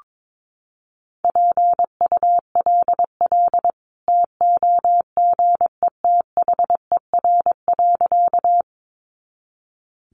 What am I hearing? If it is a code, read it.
Morse code, heard as a keyed tone, about 22 wpm, 700 Hz. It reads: PULL TOGETHER.
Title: Sentences from Top 400 Words 22wpm